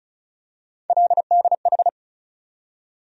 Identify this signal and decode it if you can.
Morse code, heard as a keyed tone, about 36 wpm, 700 Hz. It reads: LDH
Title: USA Call Sign Postfixes - 3 Letters 35wpm